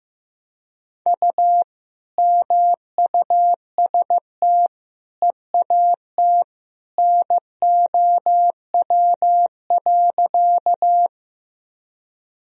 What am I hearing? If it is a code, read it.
Morse code, heard as a keyed tone, about 16 wpm, 700 Hz. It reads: U MUST EAT NOW.
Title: Sentences from Top 300 Words and Abbreviations 15wpm